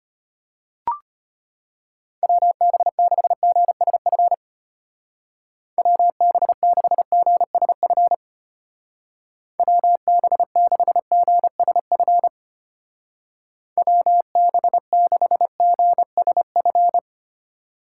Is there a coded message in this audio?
Yes: Morse code from 2.2 to 17.0 s, heard as a keyed tone, about 33 wpm, 700 Hz. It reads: WB6GSF WB6GSF WB6GSF WB6GSF